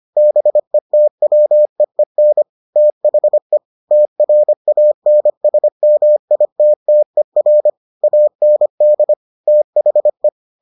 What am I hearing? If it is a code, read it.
Morse code, heard as a keyed tone, about 25 wpm, 600 Hz. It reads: BETWEEN THE TRANSMITTER AND THE